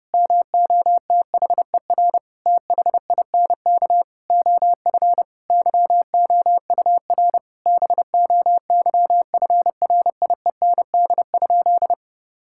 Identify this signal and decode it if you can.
Morse code, heard as a keyed tone, about 30 wpm, 700 Hz. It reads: MOTHER THINK OF YOUR BOYFRIEND?
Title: Sentences from Top 1000 Words - Review Part 5 - 30wpm